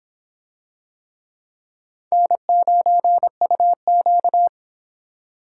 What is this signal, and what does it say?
Morse code, heard as a keyed tone, about 26 wpm, 700 Hz. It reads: N9UQ